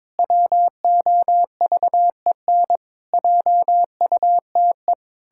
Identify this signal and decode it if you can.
Morse code, heard as a keyed tone, about 22 wpm, 700 Hz. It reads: WOVEN JUTE
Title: Pangram Challenge v2 - 22wpm